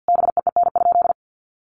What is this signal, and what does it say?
Morse code, heard as a keyed tone, about 50 wpm, 700 Hz. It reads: BEEN?
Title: Sentences from Top 400 Words 50wpm